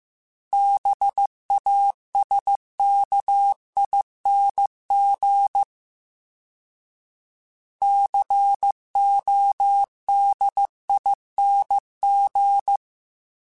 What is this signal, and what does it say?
Morse code, heard as a keyed tone, about 15 wpm, 800 Hz. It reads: BASKING CODING
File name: Special INGs-mp3s / Special INGs-mp3s / Sngl ing @13